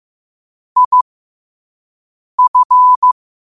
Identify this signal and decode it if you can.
Morse code, heard as a keyed tone, about 15 wpm, spaced as for 5 wpm, 1000 Hz. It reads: IF
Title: 5 WPM morse code quotes for Fri, 01 Mar 2024